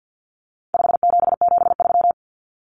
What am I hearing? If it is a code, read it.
Morse code, heard as a keyed tone, about 50 wpm, 700 Hz. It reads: 5773
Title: Common Cut Numbers - 4-Digit 50wpm